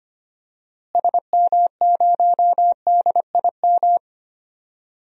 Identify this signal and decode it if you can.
Morse code, heard as a keyed tone, about 25 wpm, 700 Hz. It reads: SM0DIM
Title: DXCC-100 Call Signs - 6 Characters - Repeated 2x - 25wpm